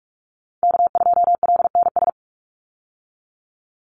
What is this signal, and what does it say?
Morse code, heard as a keyed tone, about 44 wpm, 700 Hz. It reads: K2LNS
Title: Call Signs Repeated 3x - 45wpm